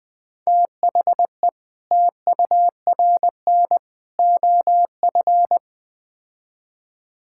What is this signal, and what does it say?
Morse code, heard as a keyed tone, about 20 wpm, 700 Hz. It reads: THE TURN OF
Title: Build A Phrase - 5-Word Phrases - 20wpm